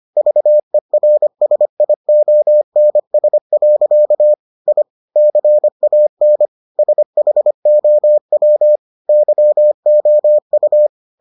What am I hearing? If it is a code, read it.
Morse code, heard as a keyed tone, about 25 wpm, 600 Hz. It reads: VERSIONS. I CAN SHOW YOU